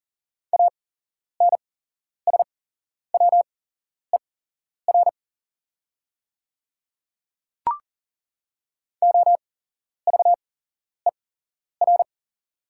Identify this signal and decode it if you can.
Morse code, heard as a keyed tone, about 40 wpm, spaced as for 11 wpm, 690 Hz. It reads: ANSWER OVER
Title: 8x Character Spacing - Top 300 Words - 40wpm